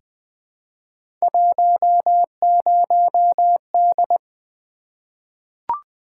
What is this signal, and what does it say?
Morse code, heard as a keyed tone, about 20 wpm, 700 Hz. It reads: 10D
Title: Three Letter-Number Combinations - 20wpm